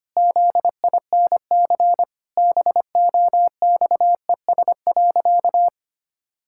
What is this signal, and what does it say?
Morse code, heard as a keyed tone, about 25 wpm, 700 Hz. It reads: ZINC BOXES.
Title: Pangram Challenge v2 - 25wpm